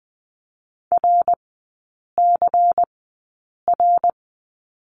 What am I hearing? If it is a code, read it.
Morse code, heard as a keyed tone, about 20 wpm, spaced as for 8 wpm, 700 Hz. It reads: RCR
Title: 093 Lesson 16e - Letter c - Three Characters 20wpm